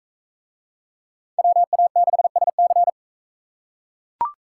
Morse code, heard as a keyed tone, about 37 wpm, 700 Hz. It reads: WA6SC